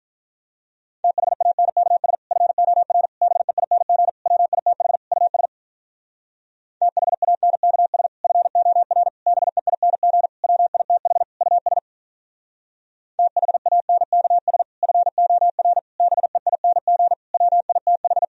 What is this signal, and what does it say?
Morse code, heard as a keyed tone, about 47 wpm, 700 Hz. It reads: THANKS FOR BEING WITH US THANKS FOR BEING WITH US THANKS FOR BEING WITH